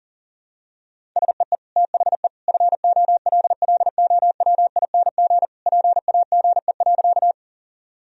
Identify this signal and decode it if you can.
Morse code, heard as a keyed tone, about 37 wpm, 700 Hz. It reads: SEE THE FOLLOWING PAGE.
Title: Sentences from Top 1000 Words - Review Part 1 - 40wpm